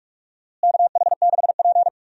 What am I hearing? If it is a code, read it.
Morse code, heard as a keyed tone, about 46 wpm, 700 Hz. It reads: KH6P
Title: DXCC-100 Call Signs - 4 Characters - Repeated 3x - 45wpm